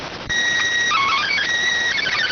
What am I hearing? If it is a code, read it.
SSTV (slow-scan television), PD 290.